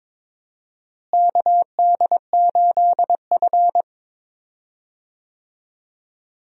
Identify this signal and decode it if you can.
Morse code, heard as a keyed tone, about 22 wpm, 700 Hz. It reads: KD8F
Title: USA Call Signs - 4 Characters - Repeated 2x - 22wpm